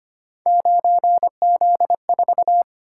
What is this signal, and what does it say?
Morse code, heard as a keyed tone, about 25 wpm, 700 Hz. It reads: 9Z4
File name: Three Letter-Number Combinations - 25wpm